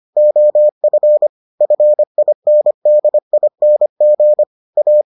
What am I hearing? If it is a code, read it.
Morse code, heard as a keyed tone, about 25 wpm, 600 Hz. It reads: OF FINDING A